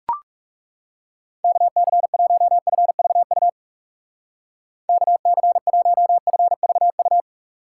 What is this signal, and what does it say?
Morse code, heard as a keyed tone, about 36 wpm, 700 Hz. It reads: KC1FVU KC1FVU
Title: Speed Racing - USA 2x3 Call Signs 30wpm